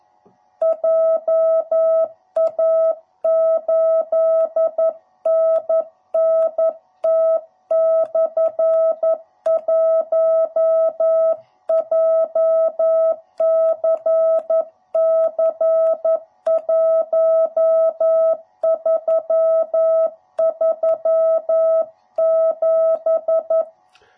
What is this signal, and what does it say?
Morse code, heard as a keyed tone, about 11 wpm, 650 Hz. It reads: JA8NNT/1 JCC1337
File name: jh8cbh / 2021cw / cw07b